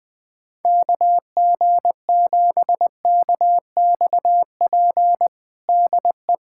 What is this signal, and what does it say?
Morse code, heard as a keyed tone, about 20 wpm, 700 Hz. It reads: KG7KXP DE
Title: USA Callsign DE Callsign - 6 Characters - 20wpm